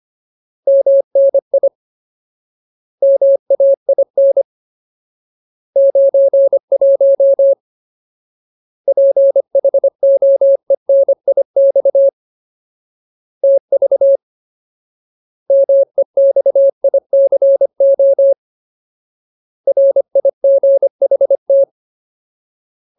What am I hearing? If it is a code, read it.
Morse code, heard as a keyed tone, about 25 wpm, 550 Hz. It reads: MNI MAIN 91 PHOENIX TV MEXICO RIGHT
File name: Ultimate Word List / 25WPM Ultimate